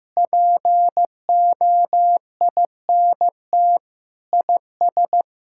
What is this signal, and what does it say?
Morse code, heard as a keyed tone, about 15 wpm, 700 Hz. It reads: POINT IS
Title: Sentences from Top 1000 Words - Review Part 5 - 15wpm